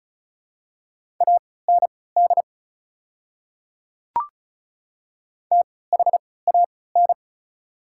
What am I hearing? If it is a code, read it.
Morse code, heard as a keyed tone, about 36 wpm, spaced as for 20 wpm, 690 Hz. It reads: AND THAN
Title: 3x Character Spacing - Top 200 Words - 35wpm